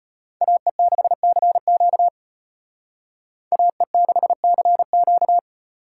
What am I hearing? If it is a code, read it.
Morse code, heard as a keyed tone, about 32 wpm, 700 Hz. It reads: AE6CQ AE6CQ